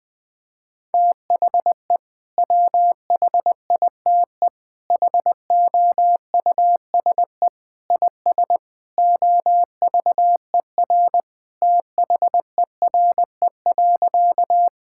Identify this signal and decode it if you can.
Morse code, heard as a keyed tone, about 20 wpm, 700 Hz. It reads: THE WHITE HOUSE IS OVER THERE.
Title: Sentences with 3-Word Phrases - Spoken Beginning - 20wpm